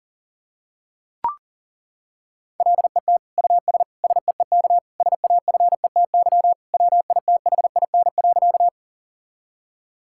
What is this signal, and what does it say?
Morse code, heard as a keyed tone, about 38 wpm, 700 Hz. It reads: LET US SEEK SAFETY WITHIN.
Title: Sentences from Top 1600 Words 40wpm